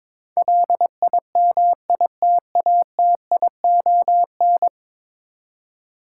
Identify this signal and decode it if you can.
Morse code, heard as a keyed tone, about 22 wpm, 700 Hz. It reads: LIMITATION
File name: practice / 22wpm / 10 Letter Words - First 3 Letters Given - 22wpm